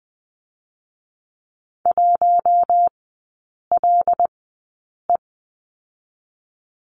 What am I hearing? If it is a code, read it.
Morse code, heard as a keyed tone, about 20 wpm, 700 Hz. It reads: 1 L E
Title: 178 Lesson 29e - Slash - Three Characters 20wpm